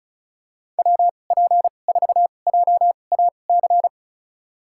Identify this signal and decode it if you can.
Morse code, heard as a keyed tone, about 34 wpm, spaced as for 25 wpm, 700 Hz. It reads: WP4JAC
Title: USA Call Signs - 6 Characters - 2x Character Spacing - 35wpm